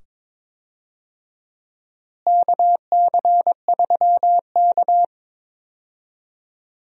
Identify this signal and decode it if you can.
Morse code, heard as a keyed tone, about 21 wpm, 700 Hz. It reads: KC3K
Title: USA Call Signs - 4 Characters - Repeated 2x - 22wpm